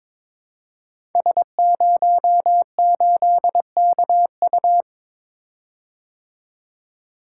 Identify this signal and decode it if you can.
Morse code, heard as a keyed tone, about 22 wpm, 700 Hz. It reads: S08KU
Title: Code Groups - 5 Letter:Numbers - 22wpm